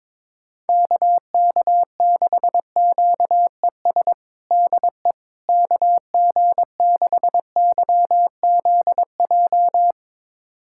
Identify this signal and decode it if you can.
Morse code, heard as a keyed tone, about 22 wpm, 700 Hz. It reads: KK6QES DE KG6YZJ
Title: USA Callsign DE Callsign - 6 Characters - 22wpm